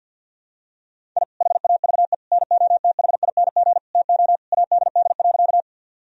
Morse code, heard as a keyed tone, about 50 wpm, 700 Hz. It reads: I HAVE NOTHING TO ADD.
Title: Sentences from Top 1000 Words - Review Part 1 - 50wpm